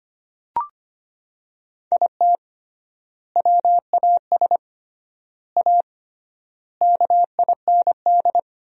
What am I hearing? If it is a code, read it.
Morse code, heard as a keyed tone, about 25 wpm, 700 Hz. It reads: IT WAS A KIND
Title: 3x Word-Spacing - Sentences from Top 500 Words - 25wpm